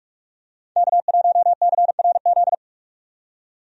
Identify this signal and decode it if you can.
Morse code, heard as a keyed tone, about 41 wpm, 700 Hz. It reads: K1CRZ
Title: USA Call Signs - 5 Characters - 45wpm